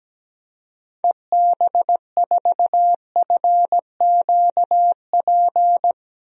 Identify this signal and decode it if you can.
Morse code, heard as a keyed tone, about 17 wpm, 700 Hz. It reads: EB4FQP
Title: DXCC-100 Call Signs - 6 Characters - 17wpm